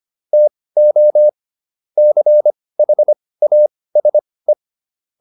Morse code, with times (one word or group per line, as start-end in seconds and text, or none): 0.3-1.3 TO
2.0-4.5 CHASE